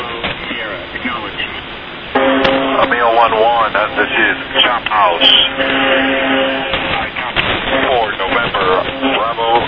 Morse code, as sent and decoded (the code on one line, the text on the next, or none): none